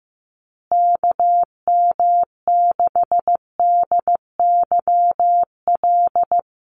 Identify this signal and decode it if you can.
Morse code, heard as a keyed tone, about 15 wpm, 700 Hz. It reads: KM6DYL
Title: USA Call Signs Repeated 3x 15wpm